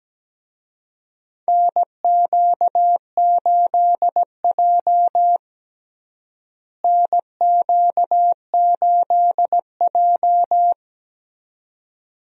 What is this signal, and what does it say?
Morse code, heard as a keyed tone, about 17 wpm, 700 Hz. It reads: NQ8J NQ8J
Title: USA Call Signs - 4 Characters - Repeated 2x - 17wpm